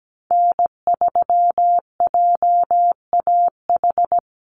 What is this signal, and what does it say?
Morse code, heard as a keyed tone, about 17 wpm, 700 Hz. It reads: N3JAH
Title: USA Call Signs 17wpm